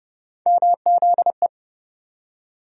Morse code, heard as a keyed tone, about 30 wpm, 700 Hz. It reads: MZE
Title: USA Call Sign Postfixes - 3 Letters 30wpm